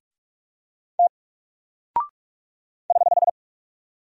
Morse code, heard as a keyed tone, about 45 wpm, 700 Hz. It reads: T <HH>
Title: Single Character (Letter:Number:Punctuation:Prosign) Alt 45wpm